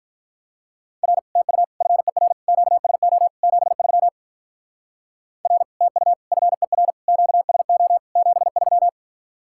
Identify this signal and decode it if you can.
Morse code, heard as a keyed tone, about 44 wpm, 700 Hz. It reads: R TU FER QSO 73 R TU FER QSO 73